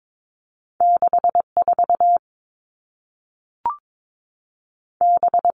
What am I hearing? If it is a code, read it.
Morse code, heard as a keyed tone, about 22 wpm, 700 Hz. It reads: 6 4 B
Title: Two Challenging Letter-Number Combinations 22wpm